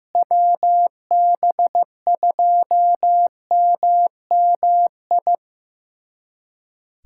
Morse code, heard as a keyed tone, about 16 wpm, 700 Hz. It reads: WB2MMI
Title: USA Callsign DE Callsign - 6 Characters - 15wpm